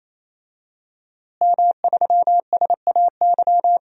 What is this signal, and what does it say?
Morse code, heard as a keyed tone, about 28 wpm, 700 Hz. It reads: M3SAY
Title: DXCC-100 Call Signs - 5 Characters - Repeated 3x - 28wpm